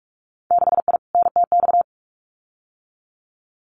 Morse code, heard as a keyed tone, about 44 wpm, 700 Hz. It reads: 6I NTX
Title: Winter Field Day Exchanges 45wpm